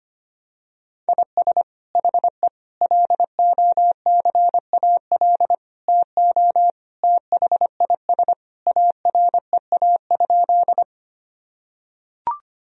Morse code, heard as a keyed tone, about 25 wpm, 700 Hz. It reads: IS HE LOCAL TO THIS AREA?
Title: Sentences from Top 1000 Words - Review Part 3 - 25wpm